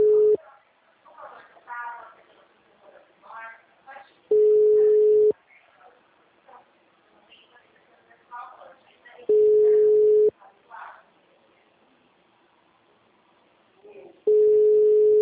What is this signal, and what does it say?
Ringback tone (Europe, 425 Hz), with speech.
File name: out- -2005- - - .40